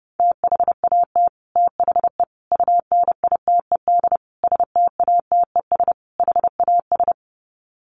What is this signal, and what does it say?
Morse code, heard as a keyed tone, about 30 wpm, 700 Hz. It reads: THAT THE UNITED STATES HAS
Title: Sets of 5 Words Encore 30wpm